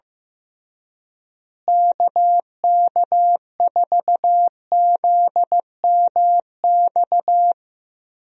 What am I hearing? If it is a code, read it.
Morse code, heard as a keyed tone, about 15 wpm, 700 Hz. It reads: KK4ZMX
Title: USA Call Signs - 6 Characters - Repeated 2x - 15wpm